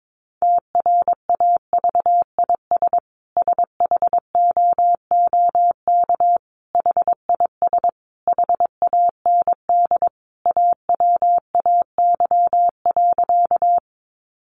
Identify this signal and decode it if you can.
Morse code, heard as a keyed tone, about 22 wpm, 700 Hz. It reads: TRAVIS SHOOK HIS HAND AWAY.
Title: A Sound Of Thunder 22wpm